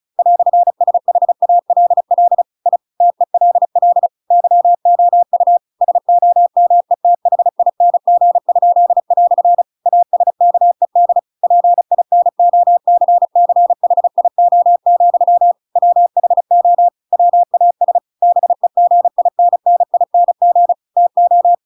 Morse code, heard as a keyed tone, about 34 wpm, 700 Hz. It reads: "SHALL I TELL YOU SOMETHING?" ASKED PINOCCHIO, WHO WAS BEGINNING TO